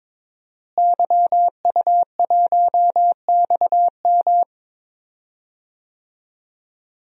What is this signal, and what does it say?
Morse code, heard as a keyed tone, about 21 wpm, 700 Hz. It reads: YU1XM
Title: DXCC-100 Call Signs - 5 Characters - Repeated 3x - 22wpm